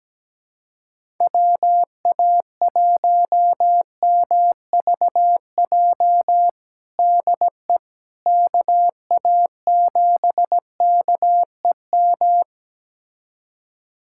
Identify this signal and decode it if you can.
Morse code, heard as a keyed tone, about 17 wpm, 700 Hz. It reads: WA1MVJ DE KA7KEM
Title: USA Callsign DE Callsign - 6 Characters - 17wpm